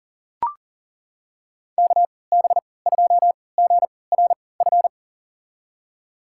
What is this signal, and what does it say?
Morse code, heard as a keyed tone, about 40 wpm, spaced as for 23 wpm, 700 Hz. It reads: KB2GRF
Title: USA Call Signs - 3x Character Spacing - 40wpm